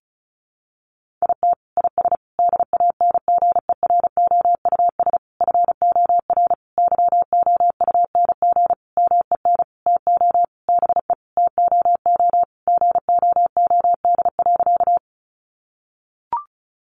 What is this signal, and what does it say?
Morse code, heard as a keyed tone, about 34 wpm, 700 Hz. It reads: IT IS DANGEROUS FOR YOUNG MEN TO BE TOO GOOD.
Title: Sentences from Top 1200 Words 35wpm